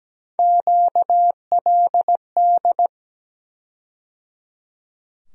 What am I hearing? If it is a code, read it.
Morse code, heard as a keyed tone, about 17 wpm, 700 Hz. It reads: QLD
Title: USA Call Sign Postfixes - 3 Letters 17wpm